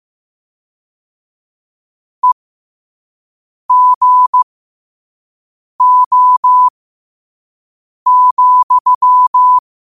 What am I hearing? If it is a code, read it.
Morse code, heard as a keyed tone, about 15 wpm, spaced as for 5 wpm, 1000 Hz. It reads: EGO,
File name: qotdfiles / 5wpm